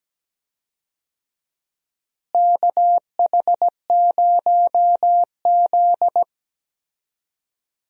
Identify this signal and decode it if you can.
Morse code, heard as a keyed tone, about 17 wpm, 700 Hz. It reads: KH0Z
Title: USA Call Signs - 4 Characters - Repeated 3x - 17wpm